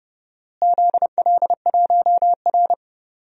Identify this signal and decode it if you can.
Morse code, heard as a keyed tone, about 31 wpm, 700 Hz. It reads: ZL1R
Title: DXCC-100 Call Signs - 4 Characters - Repeated 3x - 30wpm